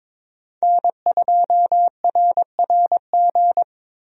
Morse code, heard as a keyed tone, about 22 wpm, 700 Hz. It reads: N2RRG
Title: USA Call Signs - 5 Characters - Repeated 3x - 22wpm